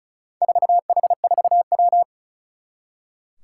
Morse code, heard as a keyed tone, about 34 wpm, 700 Hz. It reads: 4H4W
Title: Four Challenging Letter-Number Combinations 35wpm